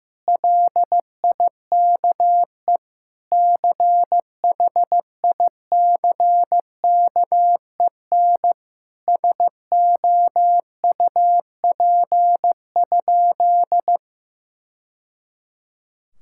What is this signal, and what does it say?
Morse code, heard as a keyed tone, about 15 wpm, 700 Hz. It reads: LIKE CHICKEN SOUP?
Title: Sentences from Top 1000 Words - Review Part 6 - 15wpm